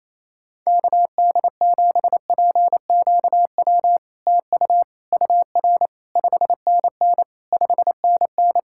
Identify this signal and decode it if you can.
Morse code, heard as a keyed tone, about 28 wpm, 700 Hz. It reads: KD7PQW TU UR 5NN 5NN
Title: SOTA QSO Practice 28wpm